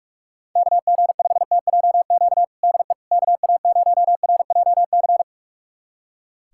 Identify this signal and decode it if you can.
Morse code, heard as a keyed tone, about 46 wpm, 700 Hz. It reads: KG5TJQ DE KA0RJC
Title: USA Callsign DE Callsign - 6 Characters - 45wpm